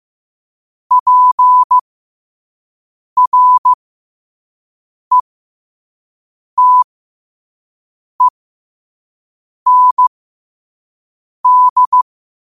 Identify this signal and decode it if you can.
Morse code, heard as a keyed tone, about 15 wpm, spaced as for 5 wpm, 1000 Hz. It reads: PRETEND
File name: qotdfiles / 5wpm